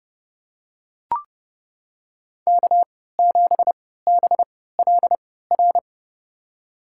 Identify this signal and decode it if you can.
Morse code, heard as a keyed tone, about 32 wpm, spaced as for 17 wpm, 700 Hz. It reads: K7BLR
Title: USA Call Signs - 5 Characters - 3x Character Spacing - 30wpm